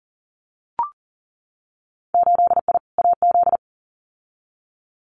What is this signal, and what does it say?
Morse code, heard as a keyed tone, about 40 wpm, 700 Hz. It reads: 8I AZ
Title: Winter Field Day Exchanges Encore 40wpm